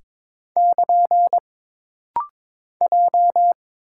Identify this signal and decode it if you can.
Morse code, heard as a keyed tone, about 22 wpm, 700 Hz. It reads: <KN> J
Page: This rapid-fire timing is geared to push your ICR (Instant Character Recognition) skills.